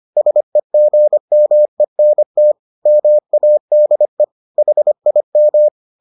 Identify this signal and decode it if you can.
Morse code, heard as a keyed tone, about 25 wpm, 600 Hz. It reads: SEGMENT MADE HIM